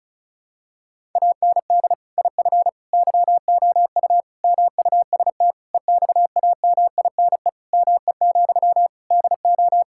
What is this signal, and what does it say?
Morse code, heard as a keyed tone, about 36 wpm, 700 Hz. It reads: AND IF YOU MUST EXAMINE ME, DO
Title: Sentences from Top 1600 Words 35wpm